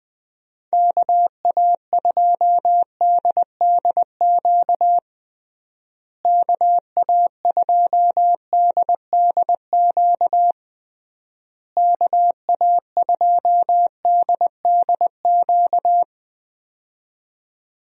Morse code, heard as a keyed tone, about 20 wpm, 700 Hz. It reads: KA2DDQ KA2DDQ KA2DDQ
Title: USA Call Signs - 6 Characters - Repeated 3x - 20wpm